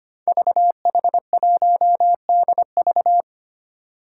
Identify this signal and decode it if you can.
Morse code, heard as a keyed tone, about 25 wpm, 700 Hz. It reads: VH1DV
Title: Five Challenging Letter-Number Combinations 25wpm